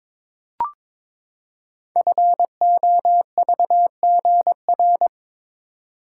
Morse code, heard as a keyed tone, about 22 wpm, 690 Hz. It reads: FOVGR
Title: Code Groups - 5 Letters - 22wpm